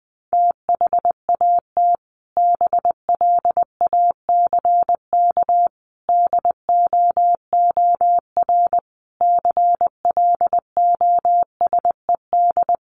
Morse code, heard as a keyed tone, about 20 wpm, 700 Hz. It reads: THAT BLACK DOOR CLOSED
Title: Sentences from Top 1400 Words 20wpm